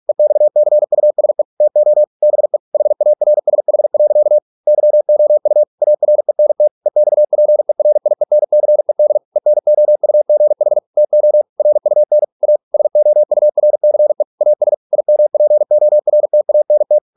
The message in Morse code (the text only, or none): EXCUSE TO BE HARSH. YOU ARENT EXPERIENCED ENOUGH TO RUN A SOURCE AS IMPORTANT